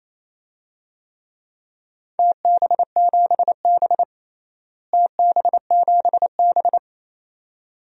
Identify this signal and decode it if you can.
Morse code, heard as a keyed tone, about 28 wpm, 700 Hz. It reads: TB7B TB7B
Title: DXCC-100 Call Signs - 4 Characters - Repeated 2x - 28wpm